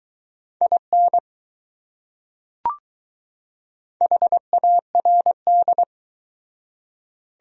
Morse code, heard as a keyed tone, about 20 wpm, 700 Hz. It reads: IN HARD